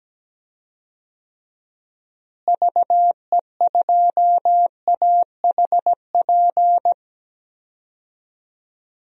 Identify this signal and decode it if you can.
Morse code, heard as a keyed tone, about 17 wpm, 700 Hz. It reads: VE2AHP
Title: DXCC-100 Call Signs - 6 Characters - Repeated 2x - 17wpm